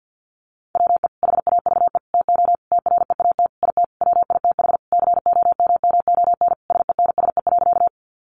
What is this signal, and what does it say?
Morse code, heard as a keyed tone, about 50 wpm, 700 Hz. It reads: WE HAVE TO TREAT IT WITH COMMON SENSE.
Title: Sentences from Top 1300 Words 50wpm